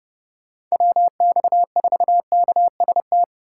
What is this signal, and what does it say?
Morse code, heard as a keyed tone, about 39 wpm, 700 Hz. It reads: WX4KST